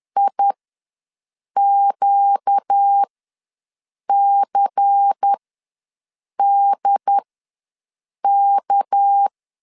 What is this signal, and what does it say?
Morse code, heard as a keyed tone, about 10 wpm, spaced as for 6 wpm, 800 Hz. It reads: IQCDK
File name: LEZIONI / 6 PKX0 / 6 PK ws 10wpm 3745